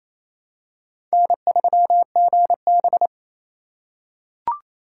Morse code, heard as a keyed tone, about 28 wpm, 700 Hz. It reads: N3GB
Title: USA Call Signs - 4 Characters - 3x Character Spacing - 28wpm